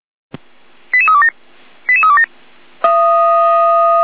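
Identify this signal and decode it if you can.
Five-tone selective call (ZVEI), heard as quick runs of five tones, tones of about 70 ms. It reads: 88327 88327
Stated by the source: ZVEI five-tone call received on SDR.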